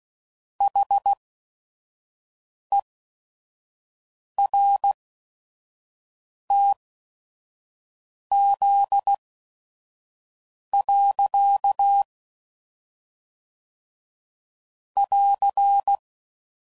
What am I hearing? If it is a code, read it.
Morse code, heard as a keyed tone, about 16 wpm, spaced as for 5 wpm, 800 Hz. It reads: HERTZ. <AR>
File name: CW Code Groups @5WPM / Technical Subjects / Reactance